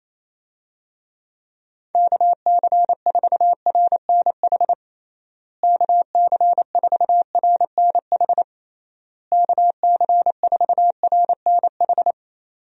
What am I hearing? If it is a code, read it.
Morse code, heard as a keyed tone, about 28 wpm, 700 Hz. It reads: KC4RNH KC4RNH KC4RNH
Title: USA Call Signs - 6 Characters - Repeated 3x - 28wpm